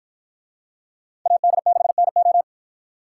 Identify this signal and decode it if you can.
Morse code, heard as a keyed tone, about 53 wpm, 700 Hz. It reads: AD6NO